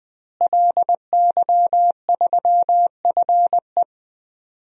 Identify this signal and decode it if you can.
Morse code, heard as a keyed tone, about 20 wpm, 700 Hz. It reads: LY3FE
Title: DXCC-100 Call Signs - 5 Characters - Repeated 2x - 20wpm